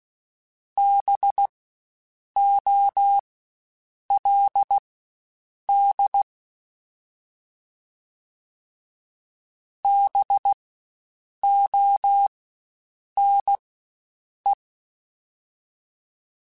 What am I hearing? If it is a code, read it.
Morse code, heard as a keyed tone, about 16 wpm, spaced as for 7 wpm, 800 Hz. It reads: BOLD BONE
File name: Spaced Alphas / Spaced Alphas mp3s / Bs sp @15